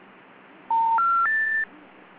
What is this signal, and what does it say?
Special information tone.